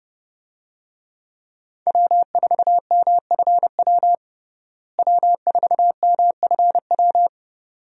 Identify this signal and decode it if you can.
Morse code, heard as a keyed tone, about 30 wpm, 700 Hz. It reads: W4MFW W4MFW
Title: USA Call Signs - 5 Characters - Repeated 2x - 30wpm